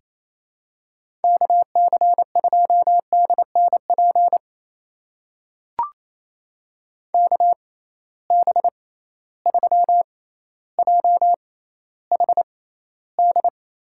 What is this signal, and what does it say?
Morse code, heard as a keyed tone, about 28 wpm, spaced as for 14 wpm, 700 Hz. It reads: KC2DNP KB3JHD
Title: USA Call Signs - 6 Characters - 6x Character Spacing - 28wpm